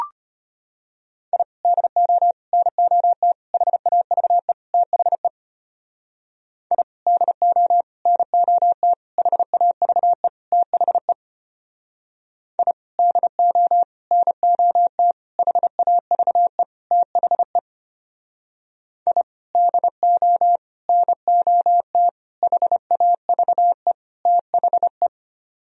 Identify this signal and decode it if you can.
Morse code, heard as a keyed tone, about 33 wpm, 700 Hz. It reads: I DO NOT HAVE THE I DO NOT HAVE THE I DO NOT HAVE THE I DO NOT HAVE THE